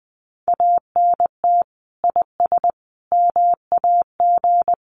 Morse code, heard as a keyed tone, about 20 wpm, 700 Hz. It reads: ANT IS MAG
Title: Common QSO Phrases 20wpm